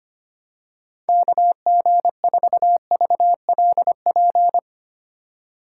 Morse code, heard as a keyed tone, about 25 wpm, 700 Hz. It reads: KG4VLP
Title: USA Call Signs - 6 Characters - 3x Character Spacing - 25wpm